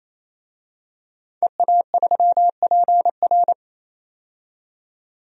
Morse code, heard as a keyed tone, about 28 wpm, 700 Hz. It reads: EA3PR
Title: DXCC-100 Call Signs - 5 Characters - 28wpm